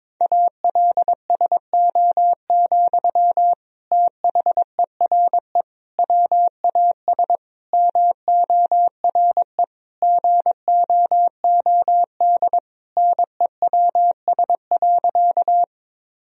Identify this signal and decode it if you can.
Morse code, heard as a keyed tone, about 22 wpm, 700 Hz. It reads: ALSO, THERE WAS MORE GOOD NEWS.
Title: Sentences from Top 1000 Words - Review Part 1 - 22wpm